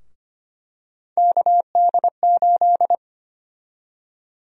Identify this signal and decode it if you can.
Morse code, heard as a keyed tone, about 25 wpm, 700 Hz. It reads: KD8
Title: USA Call Sign Prefixes - 25wpm